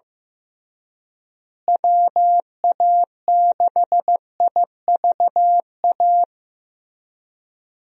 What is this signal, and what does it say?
Morse code, heard as a keyed tone, about 16 wpm, 700 Hz. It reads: WA6IVA
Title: USA Call Signs - 6 Characters - Repeated 3x - 15wpm